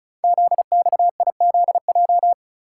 Morse code, heard as a keyed tone, about 36 wpm, 690 Hz. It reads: ZXIZJ
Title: Code Groups - 5 Letters - 35wpm